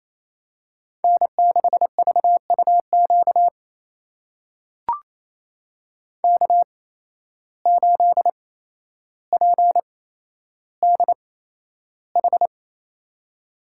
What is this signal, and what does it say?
Morse code, heard as a keyed tone, about 28 wpm, spaced as for 12 wpm, 700 Hz. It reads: N6VUQ K8PDH
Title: USA Call Signs - 5 Characters - 8x Character Spacing - 28wpm